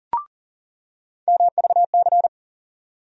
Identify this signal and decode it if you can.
Morse code, heard as a keyed tone, about 40 wpm, 700 Hz. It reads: MVC
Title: USA Call Sign Postfixes - 3 Letters 40wpm